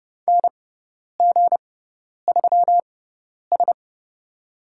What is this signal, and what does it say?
Morse code, heard as a keyed tone, about 30 wpm, spaced as for 10 wpm, 700 Hz. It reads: NG3S
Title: USA Call Signs - 6x Character Spacing - 30wpm